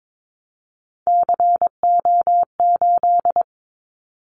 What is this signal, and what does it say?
Morse code, heard as a keyed tone, about 22 wpm, 700 Hz. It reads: CO8
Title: Callsign Prefixes for Top 100 DXCC - 22wpm